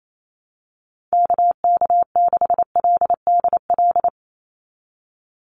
Morse code, heard as a keyed tone, about 28 wpm, 700 Hz. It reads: KK6LDL
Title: USA Call Signs 28wpm